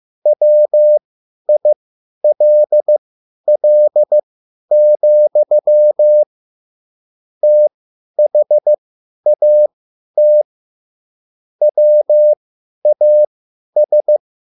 Morse code, heard as a keyed tone, about 15 wpm, spaced as for 10 wpm, 600 Hz. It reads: WILL, THAT WAS
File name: QSL / CW lesson